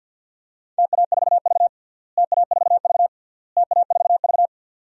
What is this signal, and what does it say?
Morse code, heard as a keyed tone, about 52 wpm, 700 Hz. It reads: TA4V TA4V TA4V
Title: DXCC-100 Call Signs - 4 Characters - Repeated 3x - 50wpm